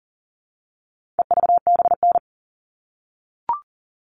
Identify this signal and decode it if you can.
Morse code, heard as a keyed tone, about 40 wpm, 700 Hz. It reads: EVBN
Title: All Cut Numbers - 4-Digit 40wpm